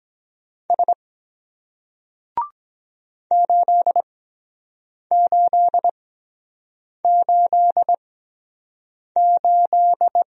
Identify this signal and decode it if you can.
Morse code, heard as a keyed tone, about 23 wpm, spaced as for 7 wpm, 700 Hz. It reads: S 8888